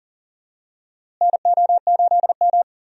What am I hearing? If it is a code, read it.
Morse code, heard as a keyed tone, about 40 wpm, 700 Hz. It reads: NO8M
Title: USA Call Signs - 4 Characters - Repeated 2x - 40wpm